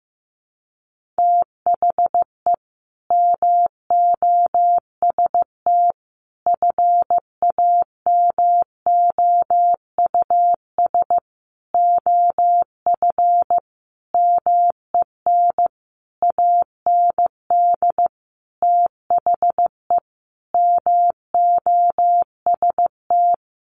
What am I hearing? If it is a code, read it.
Morse code, heard as a keyed tone, about 15 wpm, 700 Hz. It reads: THE MOST FAMOUS OF MEN AND THE MOST
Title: Sentences from Top 1500 Words 15wpm